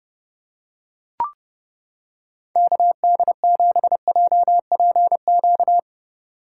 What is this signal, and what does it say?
Morse code, heard as a keyed tone, about 29 wpm, 700 Hz. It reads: KD7JPQ